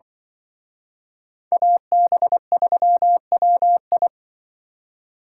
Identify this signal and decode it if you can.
Morse code, heard as a keyed tone, about 25 wpm, 700 Hz. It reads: AB3WI
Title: Speed Racing - USA 2x2 Call Signs 20wpm